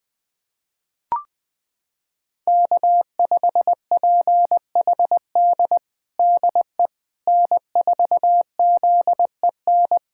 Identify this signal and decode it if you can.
Morse code, heard as a keyed tone, about 20 wpm, 700 Hz. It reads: K5PHD DE N4ZEN
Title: USA Callsign DE Callsign - 5 Characters - 20wpm